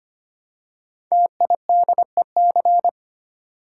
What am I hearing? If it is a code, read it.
Morse code, heard as a keyed tone, about 25 wpm, 700 Hz. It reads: TIDEC
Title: Code Groups - 5 Letters - 25wpm